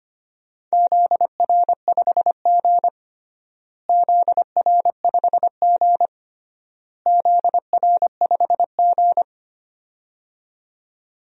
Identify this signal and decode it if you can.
Morse code, heard as a keyed tone, about 25 wpm, 700 Hz. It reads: ZR5G ZR5G ZR5G
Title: DXCC-100 Call Signs - 4 Characters - Repeated 3x - 25wpm